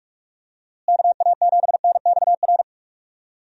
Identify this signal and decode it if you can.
Morse code, heard as a keyed tone, about 44 wpm, 700 Hz. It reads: KA7NXR
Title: USA Call Signs - 6 Characters - Repeated 2x - 45wpm